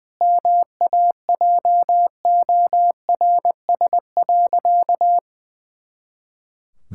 Morse code, heard as a keyed tone, about 20 wpm, 700 Hz. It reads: MAJORS.
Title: Pangram Challenge v2 - 20wpm